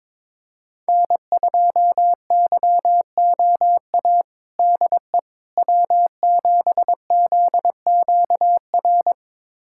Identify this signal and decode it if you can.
Morse code, heard as a keyed tone, about 22 wpm, 700 Hz. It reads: N2YOA DE W7ZQR
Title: USA Callsign DE Callsign - 5 Characters - 22wpm